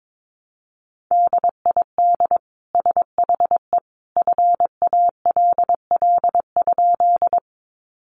Morse code, heard as a keyed tone, about 22 wpm, 700 Hz. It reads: DID SHE FALL?
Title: Sentences from Top 500 Words 22wpm